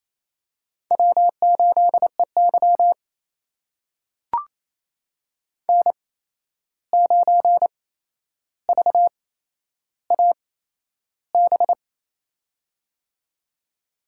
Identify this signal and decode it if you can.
Morse code, heard as a keyed tone, about 28 wpm, spaced as for 11 wpm, 700 Hz. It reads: W8EY N9VAB
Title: USA Call Signs - 8x Character Spacing - 28wpm